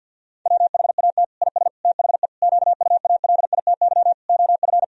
Morse code, heard as a keyed tone, about 50 wpm, 700 Hz. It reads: WHAT IS THE QUALITY OF
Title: Sentences from Top 1000 Words - Review Part 4 - 50wpm